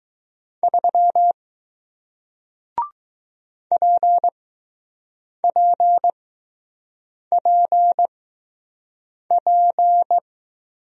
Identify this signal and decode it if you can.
Morse code, heard as a keyed tone, about 20 wpm, spaced as for 6 wpm, 700 Hz. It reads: 3 PPPP